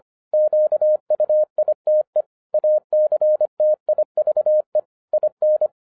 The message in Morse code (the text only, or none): QUITE ACTIVE IN